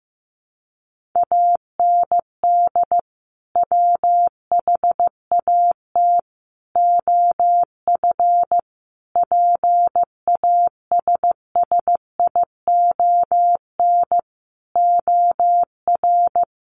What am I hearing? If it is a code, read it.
Morse code, heard as a keyed tone, about 15 wpm, 700 Hz. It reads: AND WHAT OF PASSION OR
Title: Sentences from Top 1400 Words 15wpm